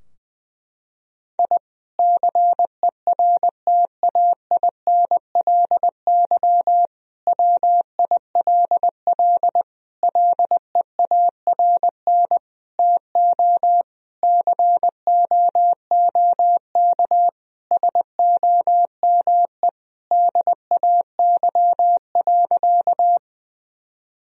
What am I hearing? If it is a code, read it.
Morse code, heard as a keyed tone, about 20 wpm, 700 Hz. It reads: I CERTAINLY WILL LEARN TO COOK SOME DAY.
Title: Sentences from Top 1000 Words - Review Part 4 - 20wpm